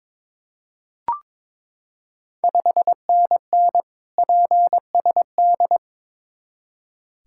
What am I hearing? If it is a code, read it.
Morse code, heard as a keyed tone, about 22 wpm, 690 Hz. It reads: 5NN PSD
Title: Ontario QSO Party Exchange 22wpm